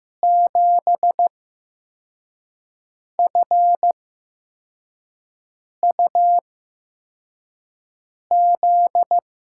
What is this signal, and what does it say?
Morse code, heard as a keyed tone, about 15 wpm, spaced as for 4 wpm, 700 Hz. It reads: 7FUZ
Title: USA Call Signs - 6 Characters - 8x Character Spacing - 15wpm